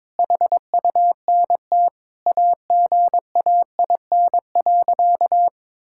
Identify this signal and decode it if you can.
Morse code, heard as a keyed tone, about 21 wpm, 700 Hz. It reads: HUNT AGAIN.
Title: Sentences from Top 1600 Words 22wpm